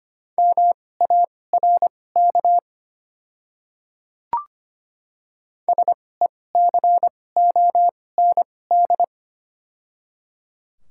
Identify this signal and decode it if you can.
Morse code, heard as a keyed tone, about 25 wpm, spaced as for 18 wpm, 700 Hz. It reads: MARK SECOND
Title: 2x Character Spacing - Top 500 Words - 25wpm